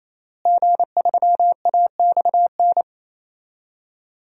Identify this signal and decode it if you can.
Morse code, heard as a keyed tone, about 28 wpm, 700 Hz. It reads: G3AXN
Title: DXCC-100 Call Signs - 5 Characters - Repeated 3x - 28wpm